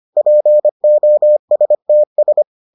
Morse code, heard as a keyed tone, about 25 wpm, 600 Hz. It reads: POSTS